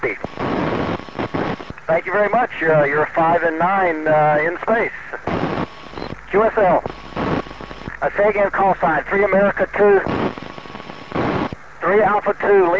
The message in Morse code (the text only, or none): none